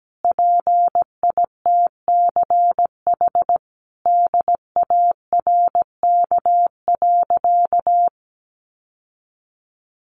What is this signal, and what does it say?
Morse code, heard as a keyed tone, about 17 wpm, 700 Hz. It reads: PITCH DARK.
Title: Sentences from Top 1300 Words 17wpm